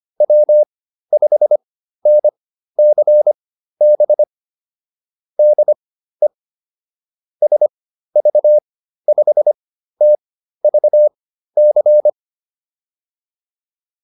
Morse code, heard as a keyed tone, about 25 wpm, spaced as for 13 wpm, 600 Hz. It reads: W5NCB DE SV5TVC
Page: intermediate-practice-files